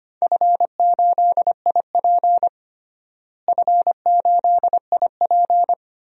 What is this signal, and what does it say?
Morse code, heard as a keyed tone, about 25 wpm, 700 Hz. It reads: F8IP F8IP
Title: DXCC-100 Call Signs - 4 Characters - Repeated 2x - 25wpm